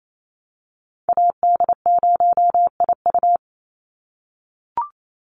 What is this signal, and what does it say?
Morse code, heard as a keyed tone, about 28 wpm, 700 Hz. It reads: AD0IU
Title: Call Signs - 28wpm